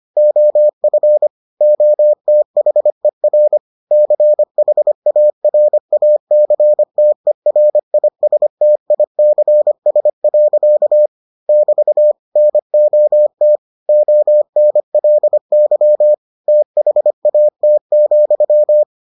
Morse code, heard as a keyed tone, about 25 wpm, 600 Hz. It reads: OF OTHER CHARACTERISTICS. <BT> NOT ONLY THAT,